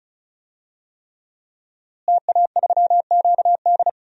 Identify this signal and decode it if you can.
Morse code, heard as a keyed tone, about 36 wpm, 700 Hz. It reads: TA3QD
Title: DXCC-100 Call Signs - 5 Characters - Repeated 3x - 35wpm